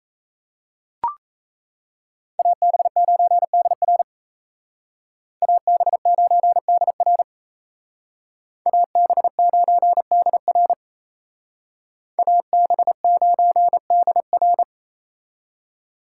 Morse code, heard as a keyed tone, about 36 wpm, 700 Hz. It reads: AB9DR AB9DR AB9DR AB9DR